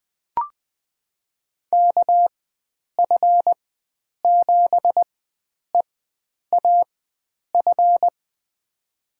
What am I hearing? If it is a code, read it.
Morse code, heard as a keyed tone, about 20 wpm, spaced as for 9 wpm, 700 Hz. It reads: KF7EAF
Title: USA Call Signs - 6 Characters - 4x Character Spacing - 20wpm